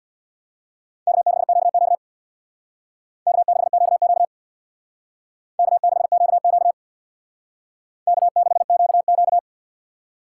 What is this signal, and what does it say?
Morse code, heard as a keyed tone, about 62 wpm, 700 Hz. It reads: K6QQ K6QQ K6QQ K6QQ
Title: Speed Racing - USA 1x2 Call Signs 50wpm